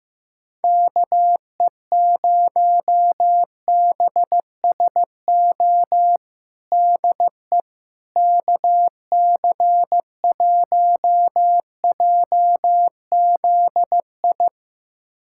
Morse code, heard as a keyed tone, about 15 wpm, 700 Hz. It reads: KE0BSO DE KC1JZI
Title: USA Callsign DE Callsign - 6 Characters - 15wpm